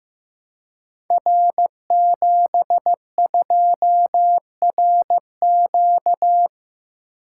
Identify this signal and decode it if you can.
Morse code, heard as a keyed tone, about 15 wpm, 700 Hz. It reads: R72RQ
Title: Code Groups - 5 Letter:Numbers - 15wpm